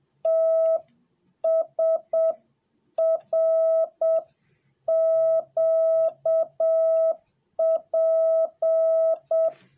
Morse code, heard as a keyed tone, about 7 wpm, 650 Hz. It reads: T S R Q P